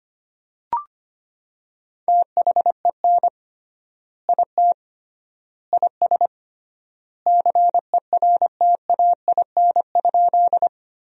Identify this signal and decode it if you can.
Morse code, heard as a keyed tone, about 25 wpm, 700 Hz. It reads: THEN IT IS CERTAIN?
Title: 3x Word-Spacing - Sentences from Top 500 Words - 25wpm